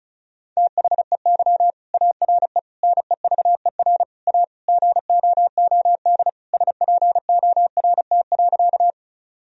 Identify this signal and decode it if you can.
Morse code, heard as a keyed tone, about 36 wpm, 700 Hz. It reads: THEY ARE NEVER A GOOD SPORT.
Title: Sentences from Top 1000 Words - Review Part 4 - 35wpm